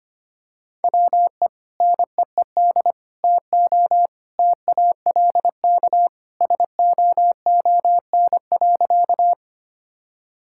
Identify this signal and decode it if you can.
Morse code, heard as a keyed tone, about 25 wpm, 700 Hz. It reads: WE NEED TO TALK SOON.
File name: practice / 25wpm / Sentences with 2-Word Phrases - 25wpm